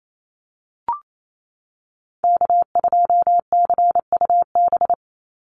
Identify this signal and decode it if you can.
Morse code, heard as a keyed tone, about 28 wpm, 700 Hz. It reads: K2CUB
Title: Call Signs - 28wpm